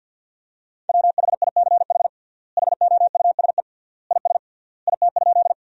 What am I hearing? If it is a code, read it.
Morse code, heard as a keyed tone, about 50 wpm, 700 Hz. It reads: WHICH HOUSE IS IT?
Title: 3x Word-Spacing - Sentences from Top 200 Words - 50wpm